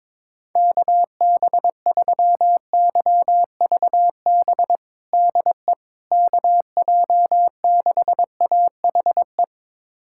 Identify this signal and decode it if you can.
Morse code, heard as a keyed tone, about 22 wpm, 700 Hz. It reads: KB3YVB DE KJ6AHE
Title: USA Callsign DE Callsign - 6 Characters - 22wpm